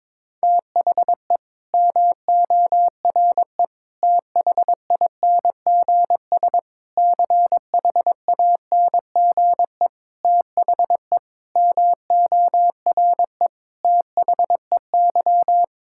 Morse code, heard as a keyed tone, about 22 wpm, 700 Hz. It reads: THE MORE THINGS CHANGE THE MORE THEY
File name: Sets of 9 Words - 22wpm